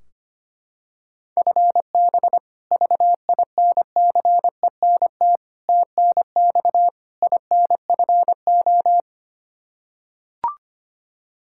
Morse code, heard as a keyed tone, about 25 wpm, 700 Hz. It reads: FB VINCENT TNX INFO
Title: Practice QSO - Exchange 3 - More Info - 25wpm